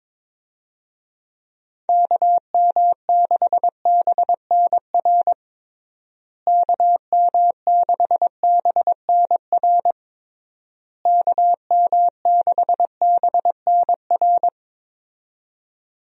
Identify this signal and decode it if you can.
Morse code, heard as a keyed tone, about 22 wpm, 700 Hz. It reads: KM6BNR KM6BNR KM6BNR
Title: USA Call Signs - 6 Characters - Repeated 3x - 22wpm